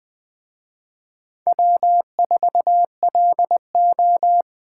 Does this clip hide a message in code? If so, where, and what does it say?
Yes: Morse code from 1.5 to 4.4 s, heard as a keyed tone, about 20 wpm, 700 Hz. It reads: W4LO